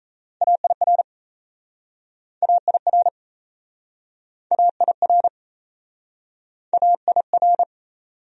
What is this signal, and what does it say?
Morse code, heard as a keyed tone, about 38 wpm, 700 Hz. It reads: AIR AIR AIR AIR